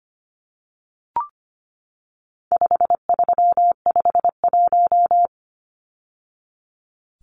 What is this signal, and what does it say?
Morse code, heard as a keyed tone, about 25 wpm, 700 Hz. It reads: 5351
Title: Numbers - 4-Digit 25wpm